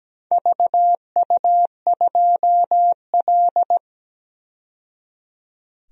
Morse code, heard as a keyed tone, about 17 wpm, 700 Hz. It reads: VU2L
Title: DXCC-100 Call Signs - 4 Characters - 17wpm